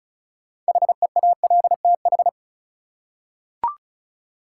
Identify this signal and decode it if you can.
Morse code, heard as a keyed tone, about 34 wpm, 700 Hz. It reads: HEALTH
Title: Six Letter Words - 35wpm